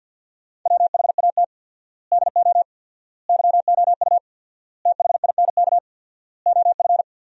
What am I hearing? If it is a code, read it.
Morse code, heard as a keyed tone, about 50 wpm, 700 Hz. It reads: WHAT DO YOU THINK OF
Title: 4x Word-Spacing - Sentences from Top 200 Words - 50wpm